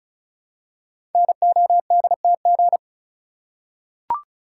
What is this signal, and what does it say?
Morse code, heard as a keyed tone, about 36 wpm, 700 Hz. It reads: NODTG
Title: Code Groups - 5 Letters - 35wpm